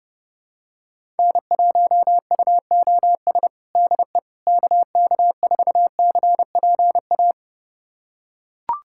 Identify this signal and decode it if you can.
Morse code, heard as a keyed tone, about 31 wpm, 700 Hz. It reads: N1UOS DE KK4CPA
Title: USA Callsign DE Callsign 30wpm